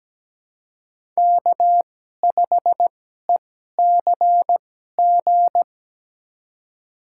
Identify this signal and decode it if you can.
Morse code, heard as a keyed tone, about 17 wpm, spaced as for 12 wpm, 700 Hz. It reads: K5ECG
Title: USA Call Signs - 2x Character Spacing - 17wpm